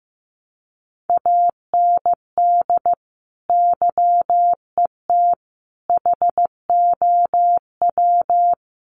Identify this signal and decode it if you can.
Morse code, heard as a keyed tone, about 15 wpm, 700 Hz. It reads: AND YET HOW
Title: Sentences from Top 1300 Words 15wpm